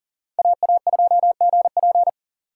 Morse code, heard as a keyed tone, about 40 wpm, 700 Hz. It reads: AA2GP
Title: USA Call Signs - 5 Characters - 3x Character Spacing - 40wpm